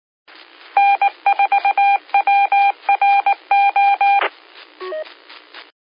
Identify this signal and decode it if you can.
Morse code, heard as a keyed tone, about 19 wpm, 800 Hz. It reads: N4WRO
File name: qsorec SimplexLogic